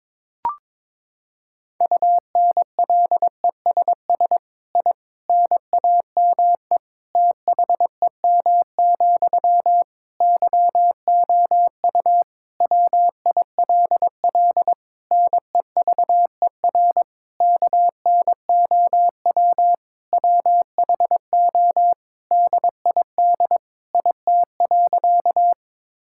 Morse code, heard as a keyed tone, about 22 wpm, 700 Hz. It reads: UNLESS I NAME THEM, YOU WILL NEVER KNOW WHO DID IT.
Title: Sentences from Top 1000 Words - Review Part 6 - 22wpm